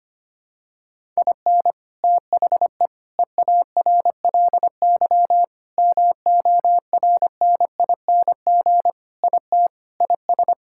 Morse code, heard as a keyed tone, about 25 wpm, 700 Hz. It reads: IN THE EARLY MORNING IT IS
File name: practice / 25wpm / Sentences with 3-Word Phrases - 25wpm